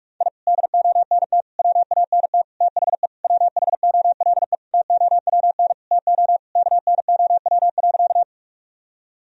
Morse code, heard as a keyed tone, about 41 wpm, 700 Hz. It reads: I DONT WANT THE WHOLE TOWN TO KNOW.
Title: Sentences from Top 1000 Words - Review Part 2 - 45wpm